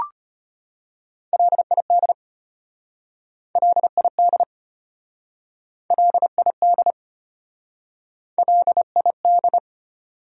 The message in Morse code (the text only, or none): LID LID LID LID